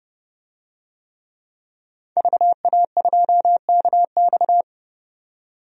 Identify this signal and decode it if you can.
Morse code, heard as a keyed tone, about 30 wpm, 700 Hz. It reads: VA2KX
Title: DXCC-100 Call Signs - 5 Characters - 30wpm